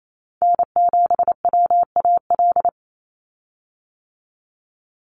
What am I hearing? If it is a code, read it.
Morse code, heard as a keyed tone, about 28 wpm, 700 Hz. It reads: N7WAL
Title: USA Call Signs 28wpm